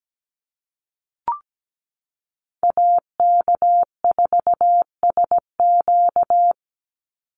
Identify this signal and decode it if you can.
Morse code, heard as a keyed tone, about 17 wpm, 700 Hz. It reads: AK4SQ
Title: Call Signs Repeated 3x - 17wpm